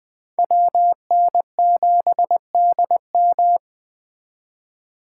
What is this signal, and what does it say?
Morse code, heard as a keyed tone, about 21 wpm, 700 Hz. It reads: WN7DM